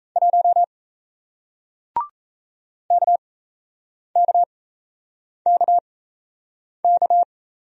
Morse code, heard as a keyed tone, about 37 wpm, spaced as for 8 wpm, 700 Hz. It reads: 1 KKKK